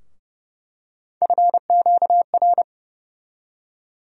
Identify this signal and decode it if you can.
Morse code, heard as a keyed tone, about 30 wpm, 700 Hz. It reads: FQR
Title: USA Call Sign Postfixes - 3 Letters 30wpm